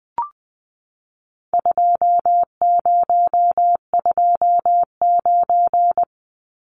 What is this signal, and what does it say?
Morse code, heard as a keyed tone, about 20 wpm, 700 Hz. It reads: 2029
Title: Numbers - 4-Digit 20wpm